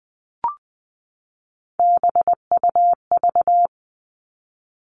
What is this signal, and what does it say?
Morse code, heard as a keyed tone, about 20 wpm, 700 Hz. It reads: BUV
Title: All Cut Numbers - 3-Digit 20wpm